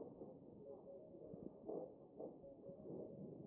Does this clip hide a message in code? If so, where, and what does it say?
No, only speech.